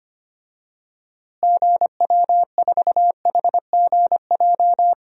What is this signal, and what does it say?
Morse code, heard as a keyed tone, about 25 wpm, 700 Hz. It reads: GW4HGJ
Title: DXCC-100 Call Signs - 6 Characters - Repeated 2x - 25wpm